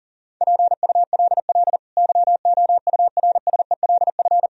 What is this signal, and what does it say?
Morse code, heard as a keyed tone, about 40 wpm, 700 Hz. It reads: PULL YOURSELF
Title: Sentences from Top 1000 Words - Review Part 2 - 40wpm